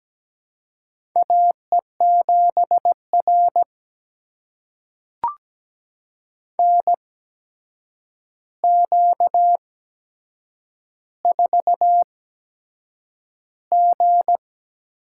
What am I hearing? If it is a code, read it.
Morse code, heard as a keyed tone, about 17 wpm, spaced as for 7 wpm, 700 Hz. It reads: AE7R NQ4G
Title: USA Call Signs - 4 Characters - 8x Character Spacing - 17wpm